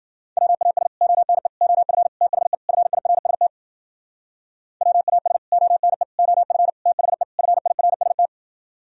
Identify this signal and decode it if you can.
Morse code, heard as a keyed tone, about 51 wpm, 700 Hz. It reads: WAS ONE OF THE FIRST WAS ONE OF THE FIRST